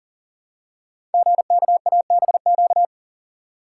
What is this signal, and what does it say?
Morse code, heard as a keyed tone, about 39 wpm, 700 Hz. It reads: GKABQ
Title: Code Groups - 5 Letters - 40wpm